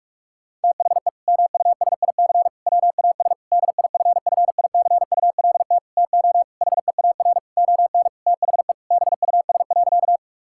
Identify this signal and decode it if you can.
Morse code, heard as a keyed tone, about 46 wpm, 700 Hz. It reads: THE MUSIC WAS DIFFICULT TO HEAR ON THE BUS.
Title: Sentences from Top 1000 Words - Review Part 6 - 45wpm